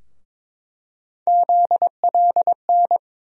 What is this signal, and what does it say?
Morse code, heard as a keyed tone, about 22 wpm, 700 Hz. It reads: ZLN
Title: USA Call Sign Postfixes - 3 Letters 22wpm